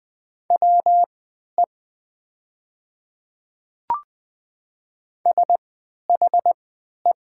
Morse code, heard as a keyed tone, about 20 wpm, spaced as for 11 wpm, 700 Hz. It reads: WE SHE
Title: 3x Character Spacing - Top 200 Words - 20wpm